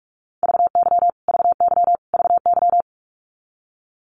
Morse code, heard as a keyed tone, about 44 wpm, 700 Hz. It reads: VY VY VY
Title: Abbreviations Repeated 3x - 45wpm